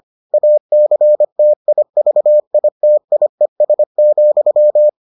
Morse code, heard as a keyed tone, about 25 wpm, 600 Hz. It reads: ACTIVITIES,